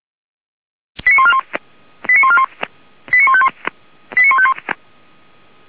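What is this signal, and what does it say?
Five-tone selective call (ZVEI), heard as quick runs of five tones, tones of about 70 ms. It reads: 89141 89141 89151 89151
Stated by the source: ZVEI five-tone selective call 89227 received by SDR, from the Feuerwehr Orlamünde fire brigade audio server.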